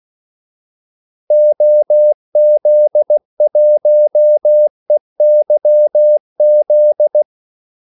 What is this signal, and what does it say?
Morse code, heard as a keyed tone, about 16 wpm, 600 Hz. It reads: OZ1EYZ
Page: Kaldesignaler 16 wpm